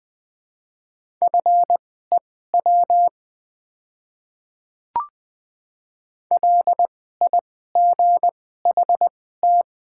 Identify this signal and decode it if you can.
Morse code, heard as a keyed tone, about 20 wpm, spaced as for 14 wpm, 700 Hz. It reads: FEW LIGHT
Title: 2x Character Spacing - Top 300 Words - 20wpm